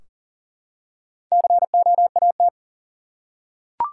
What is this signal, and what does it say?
Morse code, heard as a keyed tone, about 39 wpm, 700 Hz. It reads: COAT
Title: Four Letter Words - 40wpm